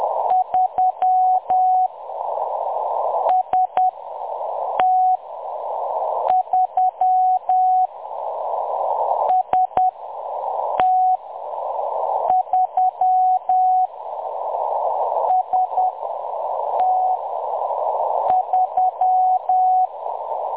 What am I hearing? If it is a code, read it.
Morse code, heard as a keyed tone, about 10 wpm, spaced as for 6 wpm, 740 Hz. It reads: 3 ST3 ST3 ST3
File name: M23-ST3- -0858utc-10184khz